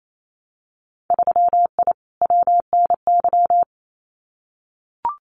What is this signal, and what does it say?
Morse code, heard as a keyed tone, about 28 wpm, 700 Hz. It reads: 3I WNY
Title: Winter Field Day Exchanges Encore 28wpm